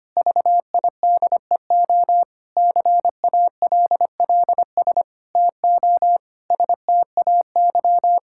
Morse code, heard as a keyed tone, about 25 wpm, 700 Hz. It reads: VIDEO CALLS TO STAY
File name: practice / 25wpm / Difficult Sentences with Hints - 25wpm